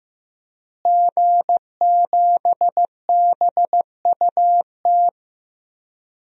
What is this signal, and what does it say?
Morse code, heard as a keyed tone, about 15 wpm, 700 Hz. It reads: G7BUT
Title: DXCC-100 Call Signs - 5 Characters - Repeated 3x - 15wpm